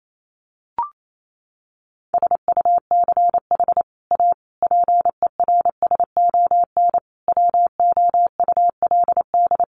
Morse code, heard as a keyed tone, about 28 wpm, 700 Hz. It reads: SUCH A PERSON WOULD
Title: Sentences from Top 1400 Words 28wpm